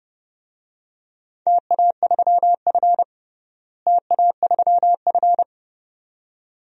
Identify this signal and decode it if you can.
Morse code, heard as a keyed tone, about 30 wpm, 700 Hz. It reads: TA3F TA3F
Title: DXCC-100 Call Signs - 4 Characters - Repeated 2x - 30wpm